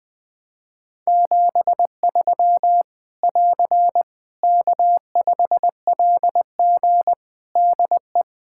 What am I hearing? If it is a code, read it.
Morse code, heard as a keyed tone, about 20 wpm, 700 Hz. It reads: 73 <AR> K5LG DE
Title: K1USN SST QSO Practice 20wpm